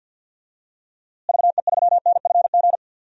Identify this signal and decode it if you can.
Morse code, heard as a keyed tone, about 51 wpm, 700 Hz. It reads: VE3NFG
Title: DXCC-100 Call Signs - 6 Characters - Repeated 3x - 50wpm